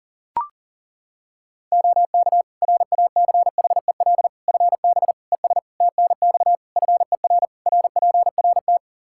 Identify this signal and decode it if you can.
Morse code, heard as a keyed tone, about 40 wpm, 700 Hz. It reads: OK RACHEL FB ES TNX FER RPRT
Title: Practice QSO - Exchange 2 - Info - 40wpm